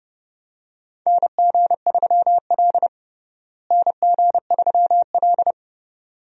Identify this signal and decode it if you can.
Morse code, heard as a keyed tone, about 32 wpm, 700 Hz. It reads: NG3L NG3L
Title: USA Call Signs - 4 Characters - Repeated 2x - 30wpm